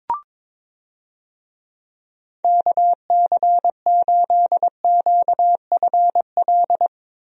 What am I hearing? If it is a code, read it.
Morse code, heard as a keyed tone, about 22 wpm, 700 Hz. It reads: KC8QFL
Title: USA Call Signs - 6 Characters - Repeated 3x - 22wpm